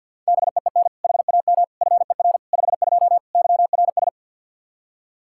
Morse code, heard as a keyed tone, about 51 wpm, 700 Hz. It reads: BEEN HAM FER 52 YRS
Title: Practice QSO - Exchange 3 - More Info - 50wpm